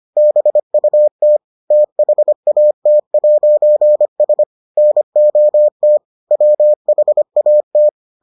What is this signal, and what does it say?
Morse code, heard as a keyed tone, about 25 wpm, 600 Hz. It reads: BUT THAT'S NOT WHAT